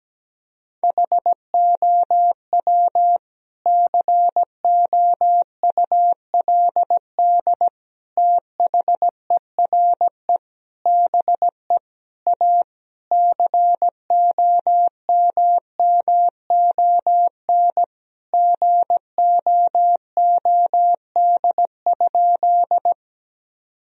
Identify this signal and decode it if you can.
Morse code, heard as a keyed tone, about 17 wpm, 700 Hz. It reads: HOW COULD THERE BE A COMMON GOOD?
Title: Sentences from Top 1000 Words - Review Part 3 - 17wpm